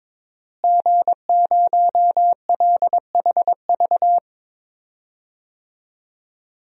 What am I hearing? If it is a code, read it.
Morse code, heard as a keyed tone, about 22 wpm, 700 Hz. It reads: G0LHV
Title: DXCC-100 Call Signs - 5 Characters - Repeated 3x - 22wpm